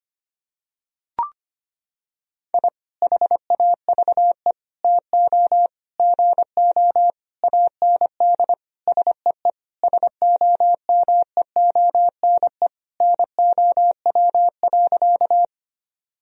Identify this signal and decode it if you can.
Morse code, heard as a keyed tone, about 25 wpm, 700 Hz. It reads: I HAVE TO GO AND SEE SOMEONE NOW.
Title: Sentences from Top 1000 Words - Review Part 4 - 25wpm